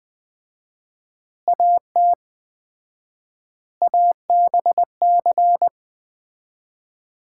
Morse code, heard as a keyed tone, about 20 wpm, 700 Hz. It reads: AT ABC
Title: 4x Word-Spacing - Sets of 5 Words - 20wpm